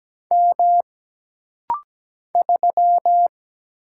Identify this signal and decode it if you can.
Morse code, heard as a keyed tone, about 17 wpm, 700 Hz. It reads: M 3
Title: Single Letter-Number - Rapid-Fire 17wpm